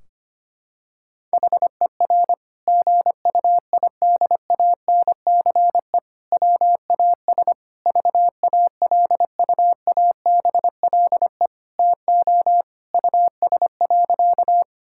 Morse code, heard as a keyed tone, about 25 wpm, 700 Hz. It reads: HER GUIDANCE WAS VALUABLE TO US.
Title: Sentences from Top 1600 Words 25wpm